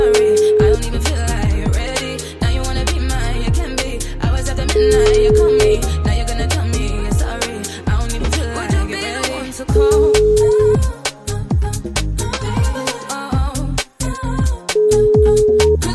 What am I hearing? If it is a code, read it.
Ringback tone (Europe, 425 Hz), with noise.